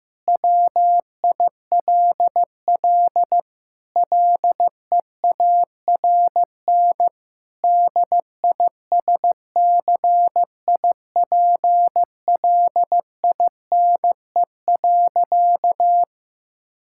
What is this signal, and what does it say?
Morse code, heard as a keyed tone, about 15 wpm, 700 Hz. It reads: WILL LEARN DISCIPLINE.
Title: Sentences from Top 1000 Words - Review Part 6 - 15wpm